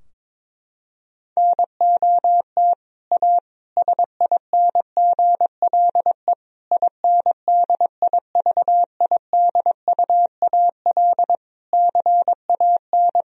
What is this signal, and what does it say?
Morse code, heard as a keyed tone, about 22 wpm, 700 Hz. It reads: NOT A SINGLE INDIVIDUAL CAN
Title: Sentences from Top 1000 Words - Review Part 4 - 22wpm